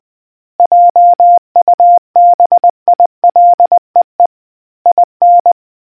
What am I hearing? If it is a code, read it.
Morse code, heard as a keyed tone, about 20 wpm, 700 Hz. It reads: JUBILEE IN